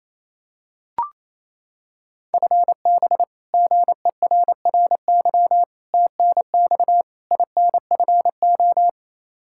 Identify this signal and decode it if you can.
Morse code, heard as a keyed tone, about 28 wpm, 700 Hz. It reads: FB GERRY TNX INFO
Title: Practice QSO - Exchange 3 - More Info - 28wpm